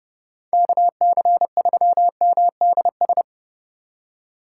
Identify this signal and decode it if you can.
Morse code, heard as a keyed tone, about 38 wpm, 700 Hz. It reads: KC3MDS